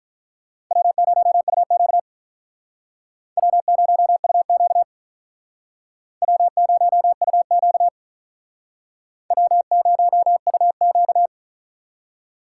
Morse code, heard as a keyed tone, about 44 wpm, 700 Hz. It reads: W0UQ W0UQ W0UQ W0UQ